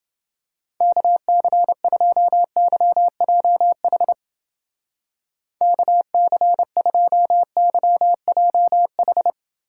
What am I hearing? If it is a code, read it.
Morse code, heard as a keyed tone, about 30 wpm, 700 Hz. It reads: KC2YJH KC2YJH